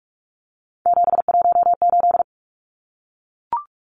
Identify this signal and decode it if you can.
Morse code, heard as a keyed tone, about 44 wpm, 700 Hz. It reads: 718
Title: Numbers - 3-Digit 45wpm